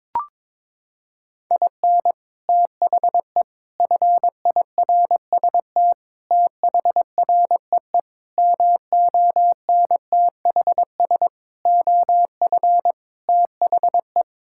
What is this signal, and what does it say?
Morse code, heard as a keyed tone, about 22 wpm, 700 Hz. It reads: IN THE FIRST THREE MONTHS OF THE
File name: practice / 22wpm / Sets of 8 Words - 22wpm